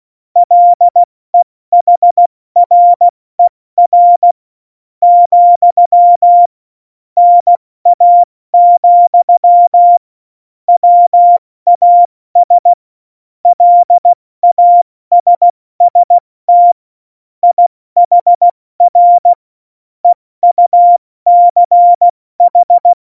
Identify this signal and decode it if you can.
Morse code, heard as a keyed tone, about 16 wpm, spaced as for 14 wpm, 700 Hz. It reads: LEHRER, NA, WAS LASST IHR EUCH
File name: Morsekurs / Witze / Witz 30. April 2019, 70 F80